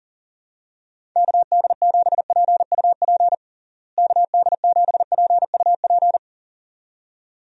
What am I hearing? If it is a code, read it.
Morse code, heard as a keyed tone, about 40 wpm, 700 Hz. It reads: KD7PUP KD7PUP
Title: USA Call Signs - 6 Characters - Repeated 2x - 40wpm